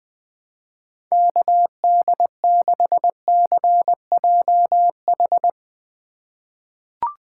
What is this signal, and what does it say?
Morse code, heard as a keyed tone, about 20 wpm, 700 Hz. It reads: KD6CJH
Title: USA Call Signs - 6 Characters - 2x Character Spacing - 20wpm